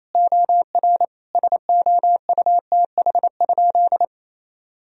Morse code, heard as a keyed tone, about 28 wpm, 700 Hz. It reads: OR SOUTH?
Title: Sentences from Top 1000 Words - Review Part 2 - 28wpm